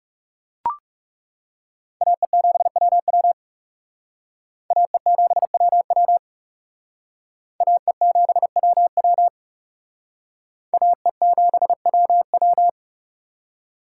Morse code, heard as a keyed tone, about 37 wpm, 700 Hz. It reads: AE7WW AE7WW AE7WW AE7WW
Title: Speed Racing - USA 2x2 Call Signs 30wpm